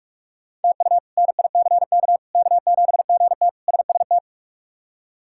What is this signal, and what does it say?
Morse code, heard as a keyed tone, about 46 wpm, 700 Hz. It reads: TU NICK K7GT SST
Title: K1USN SST QSO Practice - 45wpm